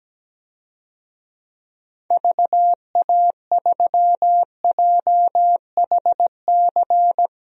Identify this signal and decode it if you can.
Morse code, heard as a keyed tone, about 17 wpm, 700 Hz. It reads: VA3JHC
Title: DXCC-100 Call Signs - 6 Characters - 17wpm